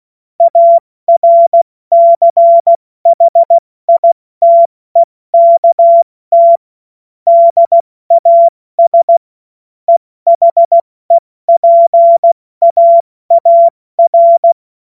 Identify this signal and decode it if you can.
Morse code, heard as a keyed tone, about 16 wpm, 680 Hz. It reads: ARCHITEKT DAS EHEPAAR
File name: Morsekurs / Witze / Witz 15. August 2023 70 F80